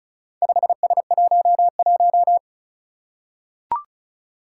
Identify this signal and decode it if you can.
Morse code, heard as a keyed tone, about 36 wpm, 700 Hz. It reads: 5S11
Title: Four Challenging Letter-Number Combinations 35wpm